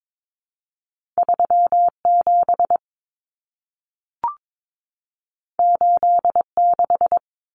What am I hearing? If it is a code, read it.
Morse code, heard as a keyed tone, about 22 wpm, 700 Hz. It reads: 37 86
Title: Common Cut Numbers - 2-Digit 22wpm